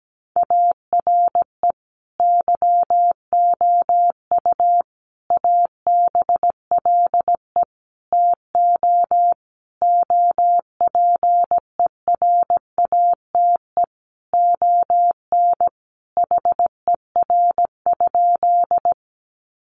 Morse code, heard as a keyed tone, about 17 wpm, 700 Hz. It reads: ARE YOU ABLE TO OPERATE ON HER?
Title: Sentences from Top 1500 Words 17wpm